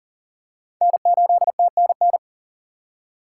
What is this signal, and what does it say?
Morse code, heard as a keyed tone, about 38 wpm, 700 Hz. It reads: N8TNN
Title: USA Call Signs - 5 Characters - 3x Character Spacing - 40wpm